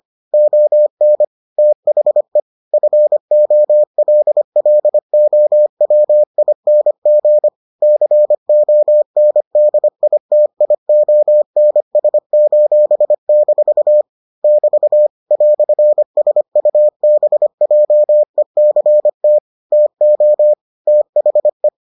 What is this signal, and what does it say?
Morse code, heard as a keyed tone, about 25 wpm, 600 Hz. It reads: ON THE FOLLOWING CONDITIONS:- <BT> "SUBJECT TO THE